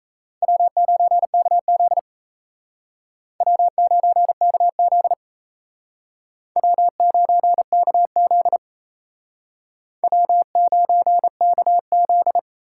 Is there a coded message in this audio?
Yes: Morse code from 0.4 to 12.4 s, heard as a keyed tone, about 37 wpm, 700 Hz. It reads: W9KZ W9KZ W9KZ W9KZ